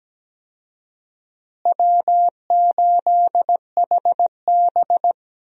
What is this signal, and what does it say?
Morse code, heard as a keyed tone, about 17 wpm, 700 Hz. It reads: W8HB
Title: USA Call Signs - 4 Characters - Repeated 3x - 17wpm